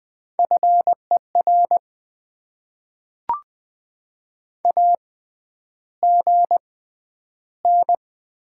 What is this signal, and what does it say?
Morse code, heard as a keyed tone, about 20 wpm, 700 Hz. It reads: F E R A G N
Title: Top 20 QSO Elements - 6x Character Spacing - 20wpm